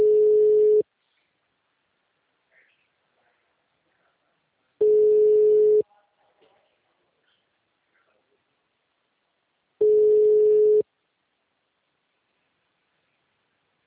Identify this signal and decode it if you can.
Ringback tone (Europe, 425 Hz), with speech.